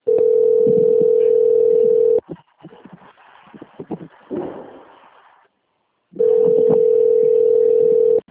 Ringback tone (North America), 440 + 480 Hz.